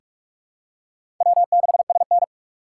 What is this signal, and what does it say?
Morse code, heard as a keyed tone, about 47 wpm, 700 Hz. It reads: W6SN
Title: USA Call Signs - 8x Character Spacing - 45wpm